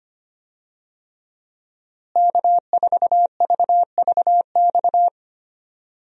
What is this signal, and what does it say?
Morse code, heard as a keyed tone, about 25 wpm, 700 Hz. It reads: K4VVX
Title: USA Call Signs - 5 Characters - Repeated 3x - 25wpm